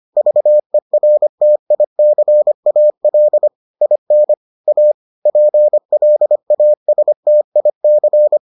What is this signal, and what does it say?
Morse code, heard as a keyed tone, about 25 wpm, 600 Hz. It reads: VERTICAL IN A PLASTIC